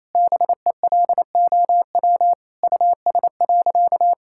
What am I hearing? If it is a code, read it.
Morse code, heard as a keyed tone, about 28 wpm, 700 Hz. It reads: BELOW US.
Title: Sentences from Top 1000 Words - Review Part 5 - 28wpm